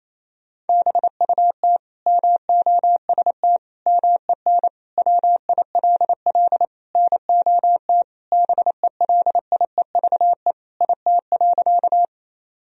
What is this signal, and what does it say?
Morse code, heard as a keyed tone, about 28 wpm, 700 Hz. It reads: BUT MOST MEN WILL NOT BELIEVE IT.
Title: Sentences from Top 1000 Words - Review Part 1 - 28wpm